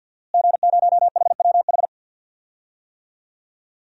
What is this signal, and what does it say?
Morse code, heard as a keyed tone, about 53 wpm, 700 Hz. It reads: G0HWH
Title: DXCC-100 Call Signs - 5 Characters - Repeated 3x - 50wpm